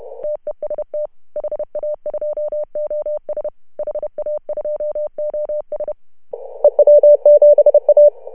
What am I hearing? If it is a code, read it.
Morse code, heard as a keyed tone, about 32 wpm, 590 Hz. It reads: TEST HA2OS HA2OS EW7A
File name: CW / HA2OS RHOB 2017 CW